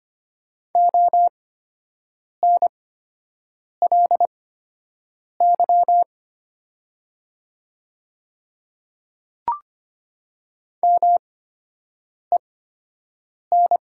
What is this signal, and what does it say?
Morse code, heard as a keyed tone, about 25 wpm, spaced as for 7 wpm, 700 Hz. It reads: ONLY MEN
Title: 8x Character Spacing - Top 200 Words - 25wpm